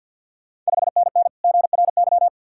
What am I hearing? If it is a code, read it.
Morse code, heard as a keyed tone, about 51 wpm, 690 Hz. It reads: 5NN GRY
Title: Ontario QSO Party Exchange 50wpm